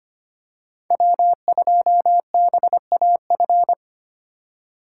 Morse code, heard as a keyed tone, about 25 wpm, 700 Hz. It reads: W2BAF
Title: USA Call Signs - 5 Characters - 2x Character Spacing - 25wpm